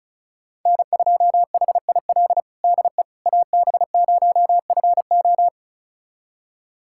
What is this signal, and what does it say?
Morse code, heard as a keyed tone, about 34 wpm, 700 Hz. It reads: N2HIL DE AB0FO
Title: USA Callsign DE Callsign - 5 Characters - 35wpm